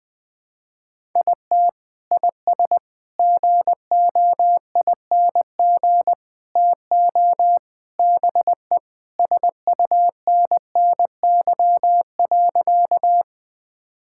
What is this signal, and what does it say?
Morse code, heard as a keyed tone, about 20 wpm, 700 Hz. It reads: IT IS GOING TO BE SUNNY.
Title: Sentences with 4-Word Phrases - 20wpm